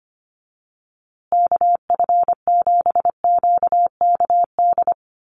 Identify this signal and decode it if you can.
Morse code, heard as a keyed tone, about 25 wpm, 700 Hz. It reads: KF7QKD
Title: USA Call Signs Repeated 2x 25wpm